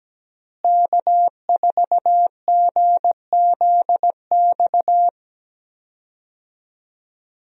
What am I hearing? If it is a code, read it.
Morse code, heard as a keyed tone, about 17 wpm, 700 Hz. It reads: K4GZX
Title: USA Call Signs - 5 Characters - Repeated 2x - 17wpm